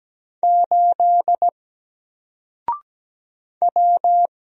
Morse code, heard as a keyed tone, about 17 wpm, 700 Hz. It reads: 8 W
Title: Single Letter-Number Alt 17wpm